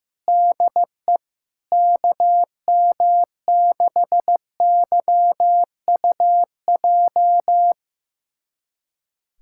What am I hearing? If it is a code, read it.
Morse code, heard as a keyed tone, about 15 wpm, 700 Hz. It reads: DE KM6YUJ
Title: USA Callsign DE Callsign 15wpm